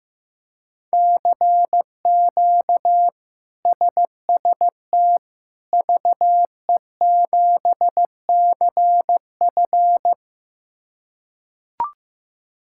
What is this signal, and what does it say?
Morse code, heard as a keyed tone, about 15 wpm, 700 Hz. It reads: CQ SST VE7CF
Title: K1USN SST QSO Practice - 15wpm